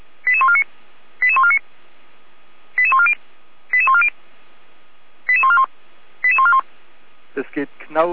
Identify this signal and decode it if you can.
Five-tone selective call (ZVEI), heard as quick runs of five tones, tones of about 70 ms, with speech. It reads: 88149 88149 88140 88140 88141 88141